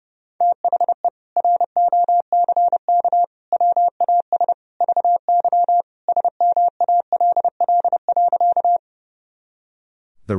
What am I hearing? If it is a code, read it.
Morse code, heard as a keyed tone, about 30 wpm, 700 Hz. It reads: THE ROCK WAS VY SMALL.
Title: Sentences from Top 400 Words and Abbreviations 30wpm